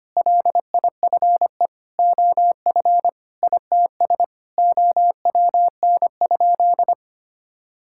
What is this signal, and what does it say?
Morse code, heard as a keyed tone, about 25 wpm, 700 Hz. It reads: LIFE OF ITS OWN?
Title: Sentences from Top 1000 Words - Review Part 5 - 25wpm